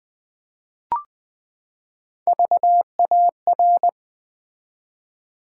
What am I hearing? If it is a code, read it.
Morse code, heard as a keyed tone, about 20 wpm, 700 Hz. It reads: VAR
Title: Build A Word - 3 Letters At A Time - 8-Letter Words - 20wpm